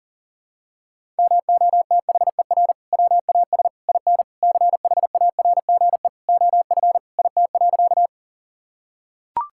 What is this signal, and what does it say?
Morse code, heard as a keyed tone, about 37 wpm, 690 Hz. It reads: MOTHER WAS IN CHARGE OF IT.
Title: Sentences from Top 1000 Words - Review Part 6 - 40wpm